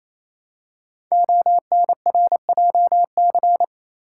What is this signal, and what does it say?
Morse code, heard as a keyed tone, about 28 wpm, 700 Hz. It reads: ONRJC
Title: Code Groups - 5 Letters - 28wpm